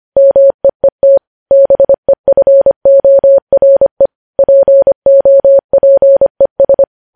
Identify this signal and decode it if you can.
Morse code, heard as a keyed tone, about 25 wpm, 550 Hz. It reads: MEET BEFORE POPES
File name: podcast / News headlines-1x-25x25